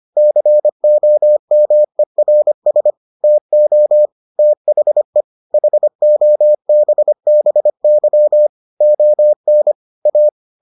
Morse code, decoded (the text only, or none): COMERS TO THE HOBBY ON A